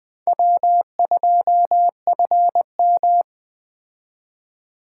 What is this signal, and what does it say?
Morse code, heard as a keyed tone, about 20 wpm, 700 Hz. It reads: W2FM
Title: Speed Racing - USA Call Signs 17wpm